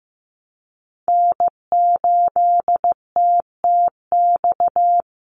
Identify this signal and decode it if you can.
Morse code, heard as a keyed tone, about 15 wpm, 700 Hz. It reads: N8TTX
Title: USA Call Signs 15wpm